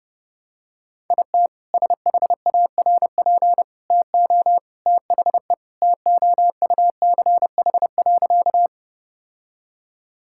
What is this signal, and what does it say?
Morse code, heard as a keyed tone, about 30 wpm, 700 Hz. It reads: IT SHARP TO THE TOUCH.
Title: Sentences from Top 1600 Words 30wpm